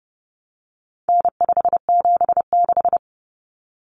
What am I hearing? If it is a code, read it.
Morse code, heard as a keyed tone, about 30 wpm, 700 Hz. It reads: N576
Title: Common Cut Numbers - 4-Digit 30wpm